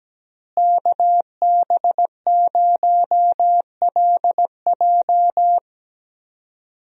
Morse code, heard as a keyed tone, about 17 wpm, 700 Hz. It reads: KB0LJ
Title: USA Call Signs - 5 Characters - Repeated 2x - 17wpm